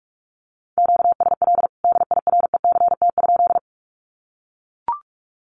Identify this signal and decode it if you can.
Morse code, heard as a keyed tone, about 44 wpm, 700 Hz. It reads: QSL DIRECT?
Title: Common QSO Phrases 45wpm